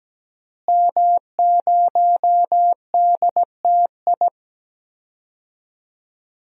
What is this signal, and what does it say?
Morse code, heard as a keyed tone, about 17 wpm, 700 Hz. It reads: M0DTI
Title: DXCC-100 Call Signs - 5 Characters - Repeated 3x - 17wpm